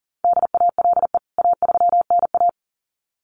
Morse code, heard as a keyed tone, about 40 wpm, 700 Hz. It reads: DALE A3NA
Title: CWT Exchange 40wpm